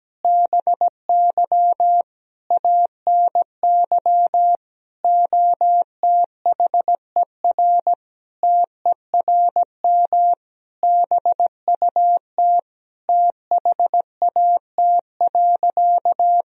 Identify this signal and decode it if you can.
Morse code, heard as a keyed tone, about 17 wpm, 700 Hz. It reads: BY ANY OTHER TERM BUT THAT.
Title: Sentences from Top 1000 Words - Review Part 4 - 17wpm